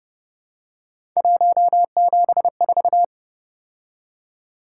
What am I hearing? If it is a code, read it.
Morse code, heard as a keyed tone, about 29 wpm, 700 Hz. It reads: 174
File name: Speed-Racing - Three Letter-Number Combinations - 25wpm